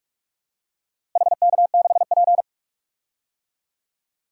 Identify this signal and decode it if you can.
Morse code, heard as a keyed tone, about 46 wpm, 700 Hz. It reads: HK6P
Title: DXCC-100 Call Signs - 4 Characters - Repeated 3x - 45wpm